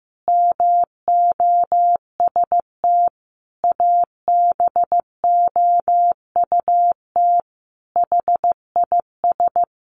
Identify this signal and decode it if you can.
Morse code, heard as a keyed tone, about 15 wpm, 700 Hz. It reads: MOST ABOUT HIS
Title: Sentences from Top 1500 Words 15wpm